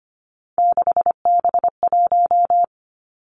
Morse code, heard as a keyed tone, about 25 wpm, 700 Hz. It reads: 6B1
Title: Three Challenging Letter-Number Combinations 25wpm